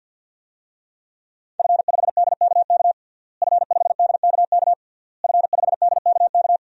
Morse code, heard as a keyed tone, about 51 wpm, 700 Hz. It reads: F5DKK F5DKK F5DKK
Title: DXCC-100 Call Signs - 5 Characters - Repeated 3x - 50wpm